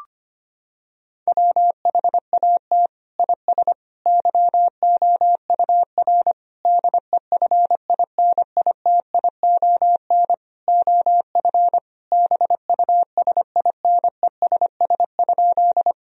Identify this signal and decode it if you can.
Morse code, heard as a keyed tone, about 25 wpm, 700 Hz. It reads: WHAT IS YOUR DEFINITION OF BUSINESS?
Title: Sentences from Top 1000 Words - Review Part 4 - 25wpm